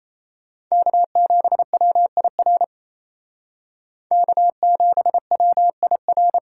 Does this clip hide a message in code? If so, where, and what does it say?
Yes: Morse code from 0.7 to 6.4 s, heard as a keyed tone, about 37 wpm, 700 Hz. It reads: K7WIR K7WIR